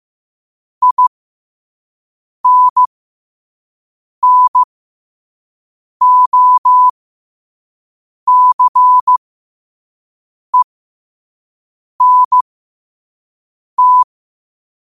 Morse code, heard as a keyed tone, about 15 wpm, spaced as for 5 wpm, 1000 Hz. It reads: INNOCENT
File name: qotdfiles / 5wpm